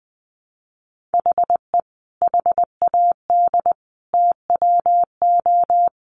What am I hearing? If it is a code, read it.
Morse code, heard as a keyed tone, about 20 wpm, 700 Hz. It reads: HE HAD TWO
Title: Sentences from Top 300 Words 20wpm